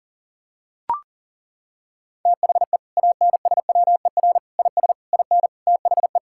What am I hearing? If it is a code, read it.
Morse code, heard as a keyed tone, about 37 wpm, 700 Hz. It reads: THE ANSWER IS IN THE
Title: Sentences from Top 1000 Words - Review Part 5 - 40wpm